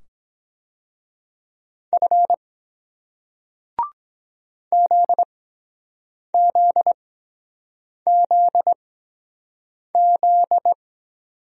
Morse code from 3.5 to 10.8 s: ZZZZ

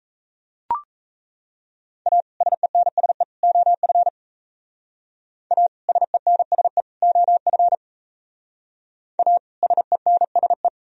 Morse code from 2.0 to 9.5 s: A SENSE OF A SENSE OF A

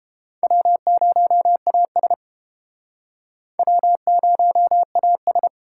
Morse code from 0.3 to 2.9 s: W0AS

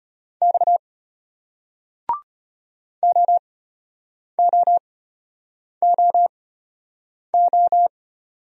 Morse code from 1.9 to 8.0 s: OOOO

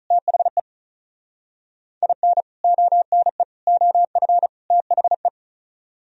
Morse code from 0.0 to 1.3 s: THE

